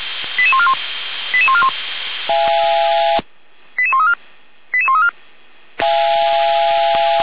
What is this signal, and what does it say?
Five-tone selective call (ZVEI), heard as quick runs of five tones, tones of about 70 ms. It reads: 88141 88141 88134 88134
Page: SDR/ZVEI